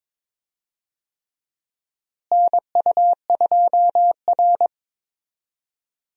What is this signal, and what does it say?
Morse code, heard as a keyed tone, about 21 wpm, 700 Hz. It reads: NU2R
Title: USA Call Signs - 4 Characters - Repeated 3x - 22wpm